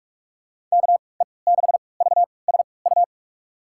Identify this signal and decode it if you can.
Morse code, heard as a keyed tone, about 46 wpm, spaced as for 25 wpm, 700 Hz. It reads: KE6VSU
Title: USA Call Signs - 6 Characters - 3x Character Spacing - 45wpm